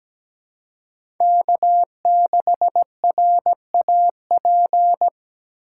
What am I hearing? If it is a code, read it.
Morse code, heard as a keyed tone, about 17 wpm, 700 Hz. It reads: K6RAP
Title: USA Call Signs - 5 Characters - 4x Character Spacing - 17wpm